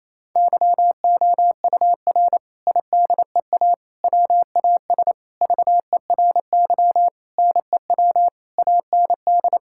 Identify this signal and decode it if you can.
Morse code, heard as a keyed tone, about 28 wpm, 700 Hz. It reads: YOUR IDEA WAS VERY NEW AND
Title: Sentences from Top 1000 Words - Review Part 6 - 28wpm